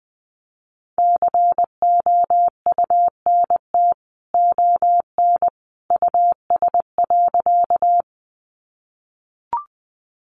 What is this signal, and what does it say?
Morse code, heard as a keyed tone, about 20 wpm, 700 Hz. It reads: COUNT ON US.
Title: Sentences from Top 1500 Words 20wpm